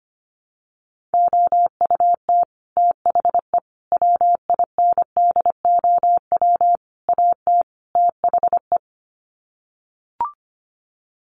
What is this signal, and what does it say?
Morse code, heard as a keyed tone, about 25 wpm, 700 Hz. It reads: OUT THE WINDOW AT THE
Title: Sets of 5 Words Encore2 25wpm